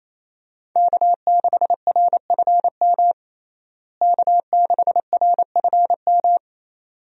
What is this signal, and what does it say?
Morse code, heard as a keyed tone, about 27 wpm, 700 Hz. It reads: K6RFM K6RFM
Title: USA Call Signs - 5 Characters - Repeated 2x - 28wpm